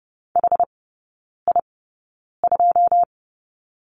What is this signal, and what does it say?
Morse code, heard as a keyed tone, about 30 wpm, spaced as for 9 wpm, 700 Hz. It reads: HI2
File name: 081 Lesson 14e - Number 2 - Three Characters 30wpm